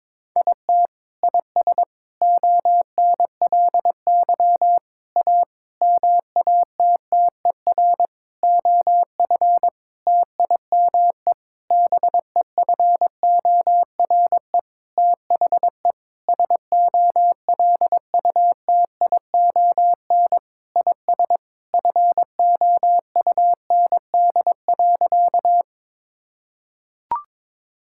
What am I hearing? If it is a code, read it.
Morse code, heard as a keyed tone, about 22 wpm, 700 Hz. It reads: IT IS ONLY A MATTER OF TIME BEFORE THE SOLUTION IS FOUND.
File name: Sentences with 9-Word Phrases - 22wpm